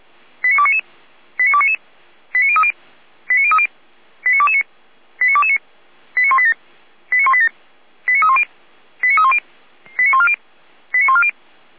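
Five-tone selective call (ZVEI), heard as quick runs of five tones, tones of about 70 ms. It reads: 89299 89299 89030 89030 89229 89229 89187 89187 89310 89310 89140 89140
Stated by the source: ZVEI five-tone selective call received with an SDR.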